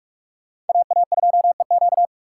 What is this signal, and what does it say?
Morse code, heard as a keyed tone, about 47 wpm, 700 Hz. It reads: AA2EQ